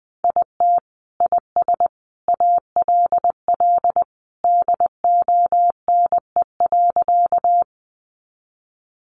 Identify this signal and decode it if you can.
Morse code, heard as a keyed tone, about 20 wpm, 700 Hz. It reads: IT IS ALL DONE.
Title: Sentences from Top 500 Words 20wpm